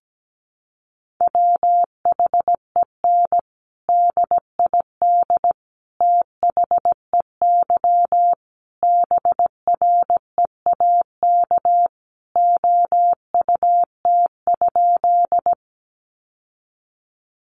Morse code, heard as a keyed tone, about 17 wpm, 700 Hz. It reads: WHEN DID THEY BREAK OUT?
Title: Sentences from Top 900 Words 17wpm